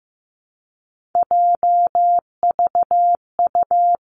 Morse code, heard as a keyed tone, about 15 wpm, 700 Hz. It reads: JVU
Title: Three Challenging Letter-Number Combinations 15wpm